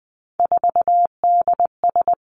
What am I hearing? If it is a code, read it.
Morse code, heard as a keyed tone, about 20 wpm, 700 Hz. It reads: 4DS
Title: Three Challenging Letter-Number Combinations 20wpm